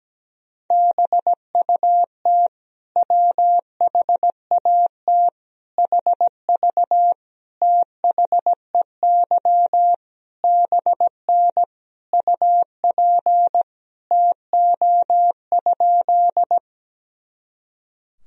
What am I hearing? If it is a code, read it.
Morse code, heard as a keyed tone, about 17 wpm, 700 Hz. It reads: BUT WHAT HV THEY BN UP TO?
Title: Sentences from Top 100 Words and Abbreviations 17wpm